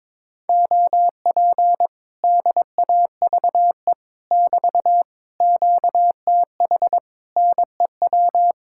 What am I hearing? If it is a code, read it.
Morse code, heard as a keyed tone, about 22 wpm, 700 Hz. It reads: OP DAVE <BT> QTH NEW
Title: Speed-Racing Common QSO Phrases - 22wpm